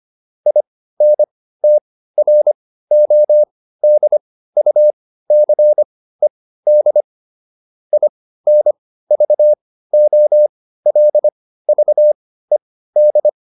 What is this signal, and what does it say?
Morse code, heard as a keyed tone, about 25 wpm, spaced as for 15 wpm, 600 Hz. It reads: INTRODUCED INVOLVED
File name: SS215 15